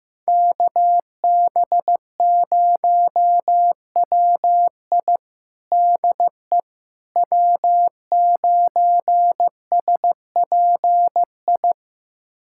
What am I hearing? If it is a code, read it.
Morse code, heard as a keyed tone, about 15 wpm, 700 Hz. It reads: KB0WI DE W9SPI
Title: USA Callsign DE Callsign - 5 Characters - 15wpm